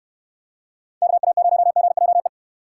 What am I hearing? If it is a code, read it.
Morse code, heard as a keyed tone, about 68 wpm, 700 Hz. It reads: KA0GJE